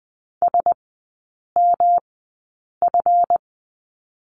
Morse code, heard as a keyed tone, about 20 wpm, spaced as for 8 wpm, 690 Hz. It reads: SMF
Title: 159 Lesson 26e - Letter g - Three Characters 20wpm